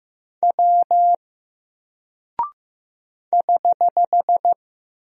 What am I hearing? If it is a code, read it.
Morse code, heard as a keyed tone, about 15 wpm, 700 Hz. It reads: W <HH>
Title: Single Character (Letter:Number:Punctuation:Prosign) Alt 15wpm